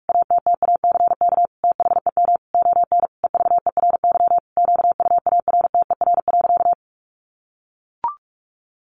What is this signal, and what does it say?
Morse code, heard as a keyed tone, about 44 wpm, 700 Hz. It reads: ATTACK THEM ON EVERY QUARTER.
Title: Sentences from Top 1500 Words 45wpm